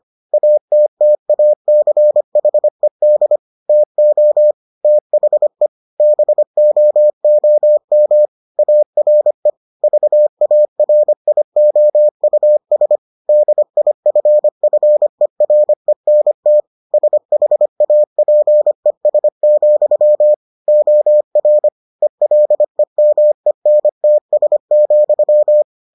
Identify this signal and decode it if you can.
Morse code, heard as a keyed tone, about 25 wpm, 600 Hz. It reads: ATTACHED TO THE BOOM ARE VARIOUS DIFFERENT SHAPES, OR ELEMENTS,